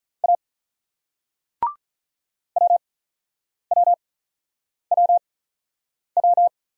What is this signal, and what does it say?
Morse code, heard as a keyed tone, about 43 wpm, 700 Hz. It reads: A W W W W
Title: Speed-Racing Single Letter-Number 35wpm